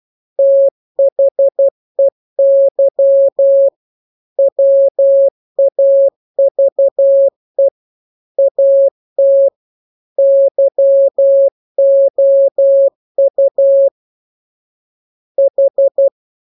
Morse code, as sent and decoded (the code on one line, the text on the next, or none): - .... -.-- / .-- .- ...- . / .- - / -.-- --- ..- / ....
THEY WAVE AT YOU H